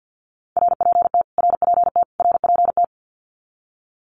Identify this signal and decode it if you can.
Morse code, heard as a keyed tone, about 49 wpm, 700 Hz. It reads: RPT RPT RPT
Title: Abbreviations Repeated 3x - 50wpm